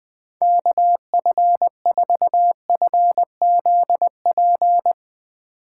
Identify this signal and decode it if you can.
Morse code, heard as a keyed tone, about 21 wpm, 700 Hz. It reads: KF4FZP
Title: Speed Racing - USA 2x3 Call Signs 20wpm